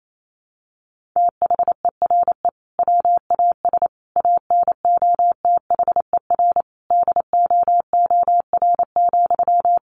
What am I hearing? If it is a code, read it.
Morse code, heard as a keyed tone, about 28 wpm, 700 Hz. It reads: THERE WAS ANOTHER DOOR,
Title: Sentences from Top 1300 Words 28wpm